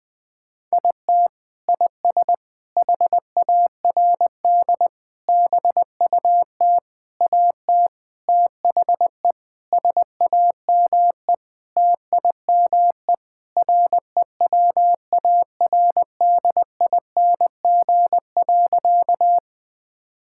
Morse code, heard as a keyed tone, about 20 wpm, 700 Hz. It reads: IT IS HARD BUT AT THE SAME TIME REWARDING.
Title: Sentences with 5-Word Phrases - 20wpm